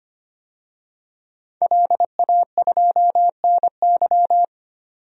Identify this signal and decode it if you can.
Morse code, heard as a keyed tone, about 25 wpm, 700 Hz. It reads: LA2NY
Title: DXCC-100 Call Signs - 5 Characters - 25wpm